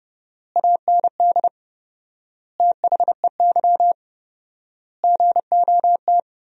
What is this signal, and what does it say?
Morse code, heard as a keyed tone, about 29 wpm, 700 Hz. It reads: AND THEY GOT
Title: 4x Word-Spacing - Sentences from Top 300 Words - 30wpm